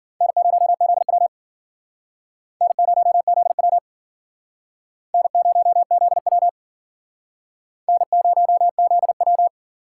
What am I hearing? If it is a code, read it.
Morse code, heard as a keyed tone, about 52 wpm, 700 Hz. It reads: N0ZW N0ZW N0ZW N0ZW